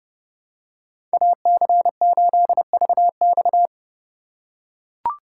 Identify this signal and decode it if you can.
Morse code, heard as a keyed tone, about 30 wpm, 700 Hz. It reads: AC8VX
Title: USA Call Signs - 3x Character Spacing - 30wpm